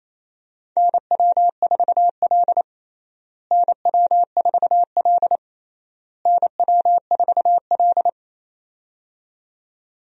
Morse code, heard as a keyed tone, about 28 wpm, 700 Hz. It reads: NW4L NW4L NW4L
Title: USA Call Signs - 4 Characters - Repeated 3x - 28wpm